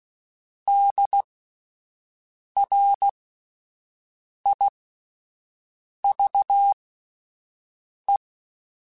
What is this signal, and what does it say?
Morse code, heard as a keyed tone, about 16 wpm, spaced as for 5 wpm, 800 Hz. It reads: DRIVE